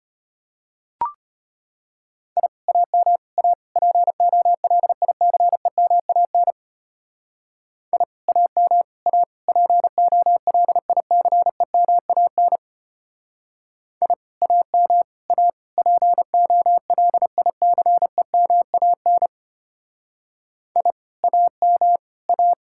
Morse code, heard as a keyed tone, about 32 wpm, 700 Hz. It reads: I AM A POLICEMAN I AM A POLICEMAN I AM A POLICEMAN I AM A